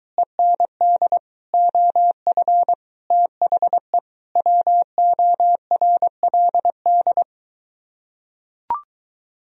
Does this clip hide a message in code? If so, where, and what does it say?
Yes: Morse code from 0.2 to 7.2 s, heard as a keyed tone, about 23 wpm, 700 Hz. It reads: END OF THE WORLD